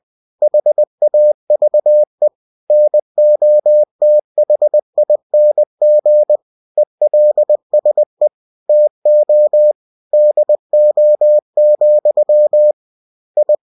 Morse code, heard as a keyed tone, about 20 wpm, 600 Hz. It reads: HAVE NOTHING ELSE TO DO, I
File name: cw-books / winnie-the-pooh / winnie-20wpm-page-0068